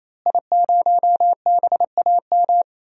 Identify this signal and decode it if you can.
Morse code, heard as a keyed tone, about 28 wpm, 700 Hz. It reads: I0BAM
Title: DXCC-100 Call Signs - 5 Characters - Repeated 3x - 28wpm